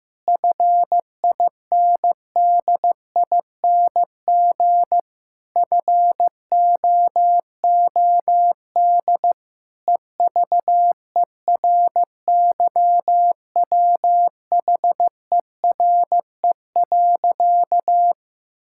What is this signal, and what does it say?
Morse code, heard as a keyed tone, about 15 wpm, 700 Hz. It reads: FINDING FOOD EVERYWHERE.
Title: Sentences from Top 1600 Words 15wpm